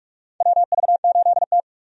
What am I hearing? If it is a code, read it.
Morse code, heard as a keyed tone, about 47 wpm, 700 Hz. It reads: WV8T
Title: USA Call Signs - 4 Characters - 45wpm